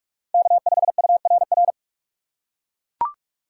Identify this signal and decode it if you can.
Morse code, heard as a keyed tone, about 42 wpm, 700 Hz. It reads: K5URR
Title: USA Call Signs - 5 Characters - 4x Character Spacing - 45wpm